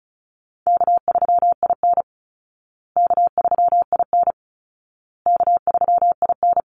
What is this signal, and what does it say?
Morse code, heard as a keyed tone, about 34 wpm, 700 Hz. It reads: K3IN K3IN K3IN
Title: Easy Call Signs Repeated 3x - 35wpm